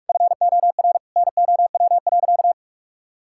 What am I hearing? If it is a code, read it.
Morse code, heard as a keyed tone, about 44 wpm, 700 Hz. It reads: FOR NOW.
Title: Sentences from Top 1000 Words - Review Part 5 - 45wpm